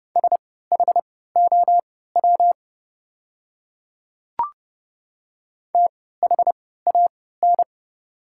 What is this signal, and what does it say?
Morse code, heard as a keyed tone, about 30 wpm, spaced as for 17 wpm, 700 Hz. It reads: SHOW THAN
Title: 3x Character Spacing - Top 200 Words - 30wpm